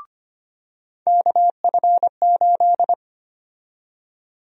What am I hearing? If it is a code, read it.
Morse code, heard as a keyed tone, about 25 wpm, 700 Hz. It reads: KF8
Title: USA Call Sign Prefixes - 25wpm